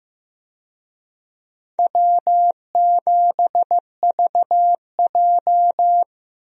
Morse code, heard as a keyed tone, about 15 wpm, 700 Hz. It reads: W7VJ
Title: USA Call Signs - 4 Characters - Repeated 2x - 15wpm